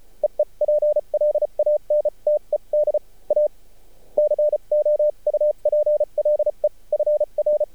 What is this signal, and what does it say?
Morse code, heard as a keyed tone, about 34 wpm, spaced as for 30 wpm, 600 Hz. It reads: EE PLANTED A COUPLE FR
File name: archived / CRSnet